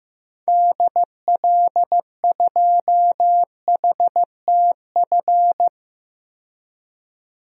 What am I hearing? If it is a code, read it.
Morse code, heard as a keyed tone, about 16 wpm, 700 Hz. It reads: DL2HTF
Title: DXCC-100 Call Signs - 6 Characters - 15wpm